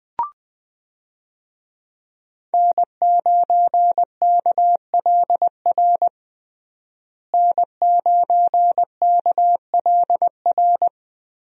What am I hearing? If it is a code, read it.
Morse code, heard as a keyed tone, about 20 wpm, 700 Hz. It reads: N9KLR N9KLR
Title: USA Call Signs - 5 Characters - Repeated 3x - 20wpm